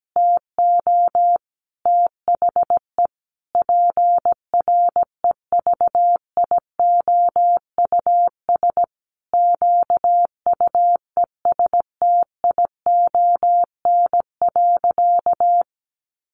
Morse code, heard as a keyed tone, about 17 wpm, 700 Hz. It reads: TO THE PREVIOUS QUESTION.
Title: Sentences from Top 1500 Words 17wpm